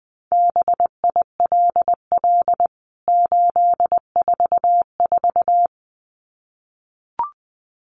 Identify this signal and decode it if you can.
Morse code, heard as a keyed tone, about 20 wpm, 700 Hz. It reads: BILL 844
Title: CWT Exchange 20wpm